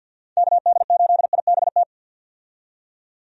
Morse code, heard as a keyed tone, about 47 wpm, 700 Hz. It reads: KD8IBT
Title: USA Call Signs - 6 Characters - 50wpm